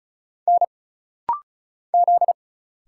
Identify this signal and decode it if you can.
Morse code, heard as a keyed tone, about 36 wpm, 700 Hz. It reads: N Z
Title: Single Letter-Number Alt - Rapid-Fire 35wpm